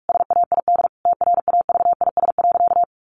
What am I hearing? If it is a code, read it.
Morse code, heard as a keyed tone, about 44 wpm, 700 Hz. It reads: SAID TRAVIS.
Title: A Sound Of Thunder 45wpm